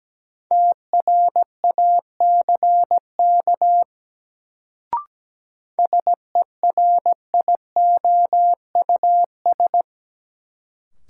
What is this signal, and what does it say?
Morse code, heard as a keyed tone, about 17 wpm, 700 Hz. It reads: TRACK SERIOUS
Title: Sentences from Top 1100 Words - New Word Drill Alt - 17wpm